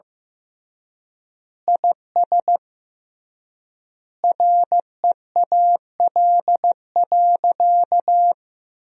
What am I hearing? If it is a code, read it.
Morse code, heard as a keyed tone, about 15 wpm, 700 Hz. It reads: IS REAL.
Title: 3x Word-Spacing - Sentences from Top 300 Words - 15wpm